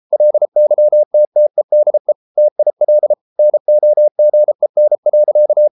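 Morse code, heard as a keyed tone, about 33 wpm, 600 Hz. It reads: LYTTEDE TIL NOGEN.
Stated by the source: Morse code at 33 wpm, QRQ.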